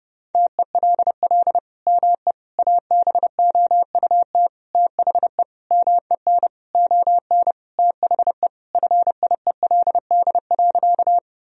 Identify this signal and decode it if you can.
Morse code, heard as a keyed tone, about 30 wpm, 700 Hz. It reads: TELL ME ABOUT THE MEN ON THE FIELD.
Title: Sentences from Top 1000 Words - Review Part 2 - 30wpm